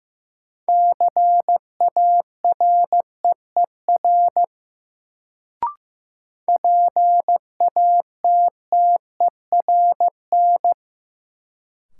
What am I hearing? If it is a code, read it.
Morse code, heard as a keyed tone, about 16 wpm, 700 Hz. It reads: CAREER PATTERN
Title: Sentences from Top 1000 Words - New Word Drill - 15wpm